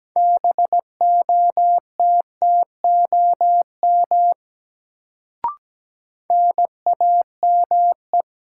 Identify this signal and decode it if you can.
Morse code, heard as a keyed tone, about 17 wpm, 700 Hz. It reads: BOTTOM NAME
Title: Sets of 3 Words - New Word Drill - 17wpm